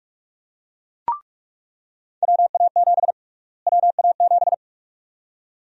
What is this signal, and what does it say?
Morse code, heard as a keyed tone, about 41 wpm, 700 Hz. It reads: WA7 WA7
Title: USA Call Sign Prefixes - Repeated 2x - 45wpm